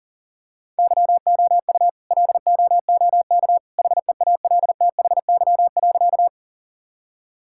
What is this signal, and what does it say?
Morse code, heard as a keyed tone, about 40 wpm, 700 Hz. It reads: YOU LOOK HEALTHY.
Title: Sentences from Top 1000 Words - Review Part 5 - 40wpm